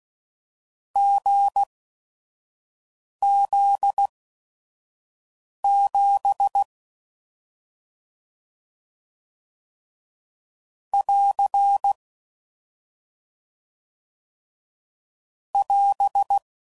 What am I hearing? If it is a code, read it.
Morse code, heard as a keyed tone, about 16 wpm, spaced as for 5 wpm, 800 Hz. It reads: GZ7 <AR> <AS>
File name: Combinations / Difficult Combos / GZ GZ7